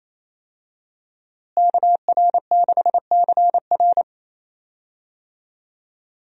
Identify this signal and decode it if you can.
Morse code, heard as a keyed tone, about 28 wpm, 700 Hz. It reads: KR6CR
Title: USA Call Signs - 5 Characters - Repeated 3x - 28wpm